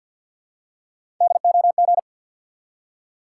Morse code, heard as a keyed tone, about 53 wpm, 700 Hz. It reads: DOG
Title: Three Letter Words - 50wpm